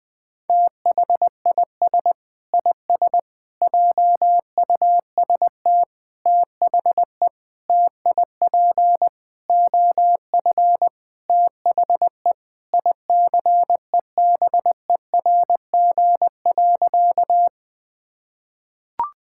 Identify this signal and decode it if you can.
Morse code, heard as a keyed tone, about 20 wpm, 700 Hz. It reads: THIS IS JUST THE TIP OF THE ICEBERG.
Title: Sentences with 6-Word Phrases - 20wpm